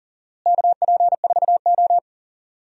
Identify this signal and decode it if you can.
Morse code, heard as a keyed tone, about 40 wpm, 700 Hz. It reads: KP4O
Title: DXCC-100 Call Signs - 4 Characters - Repeated 3x - 40wpm